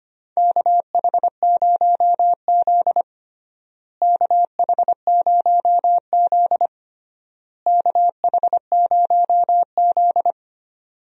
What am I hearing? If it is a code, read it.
Morse code, heard as a keyed tone, about 25 wpm, 700 Hz. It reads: KH0Z KH0Z KH0Z
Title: USA Call Signs - 4 Characters - Repeated 3x - 25wpm